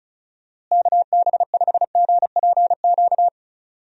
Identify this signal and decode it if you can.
Morse code, heard as a keyed tone, about 34 wpm, 700 Hz. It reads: KB5GPQ
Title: USA Call Signs - 6 Characters - Repeated 3x - 35wpm